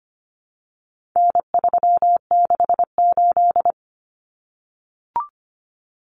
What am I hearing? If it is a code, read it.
Morse code, heard as a keyed tone, about 25 wpm, 700 Hz. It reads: N368
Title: Common Cut Numbers - 4-Digit 25wpm